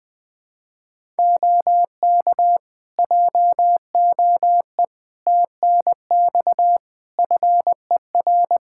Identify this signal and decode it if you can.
Morse code, heard as a keyed tone, about 20 wpm, 700 Hz. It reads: OK JOE TNX FER
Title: Practice QSO - Exchange 4 - Ending - 20wpm